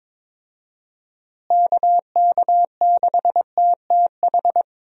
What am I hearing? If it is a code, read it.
Morse code, heard as a keyed tone, about 22 wpm, 700 Hz. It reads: KK6TTH
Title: USA Call Signs - 6 Characters - Repeated 2x - 22wpm